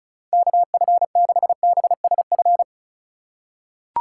Morse code, heard as a keyed tone, about 36 wpm, 700 Hz. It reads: KF6BSF
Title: USA Call Signs - 6 Characters - 4x Character Spacing - 35wpm